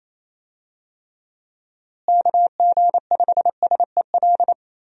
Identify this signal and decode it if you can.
Morse code, heard as a keyed tone, about 28 wpm, 700 Hz. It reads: KG5SEL
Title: USA Call Signs - 6 Characters - Repeated 3x - 28wpm